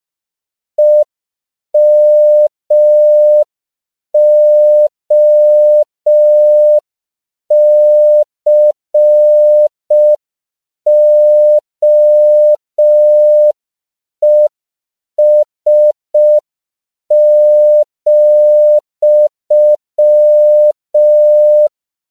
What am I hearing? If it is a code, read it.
Morse code, heard as a keyed tone, about 5 wpm, 600 Hz. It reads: EMOCOES,
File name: CW / 5PPM